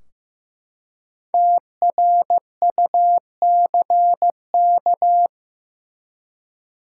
Morse code, heard as a keyed tone, about 15 wpm, 700 Hz. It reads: TRUCK
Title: Five Letter Words - 15wpm